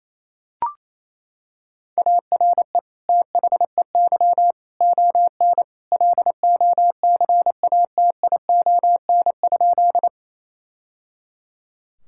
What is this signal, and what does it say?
Morse code, heard as a keyed tone, about 28 wpm, 700 Hz. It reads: ARE THEY ON LOCATION?
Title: Sentences from Top 1000 Words - Review Part 6 - 28wpm